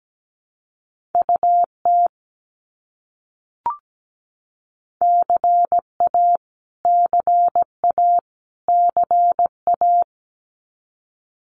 Morse code, heard as a keyed tone, about 17 wpm, 700 Hz. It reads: UT CA CA CA
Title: US States Repeated 3x 17wpm